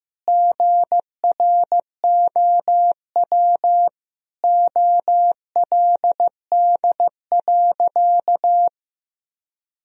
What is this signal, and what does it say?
Morse code, heard as a keyed tone, about 15 wpm, 700 Hz. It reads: GROW OLD.
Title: Sentences from Top 1000 Words - Review Part 1 - 15wpm